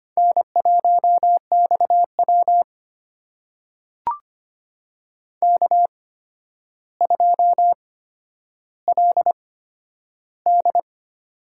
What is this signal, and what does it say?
Morse code, heard as a keyed tone, about 25 wpm, spaced as for 11 wpm, 700 Hz. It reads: N1XW K2LD
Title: USA Call Signs - 4 Characters - 8x Character Spacing - 25wpm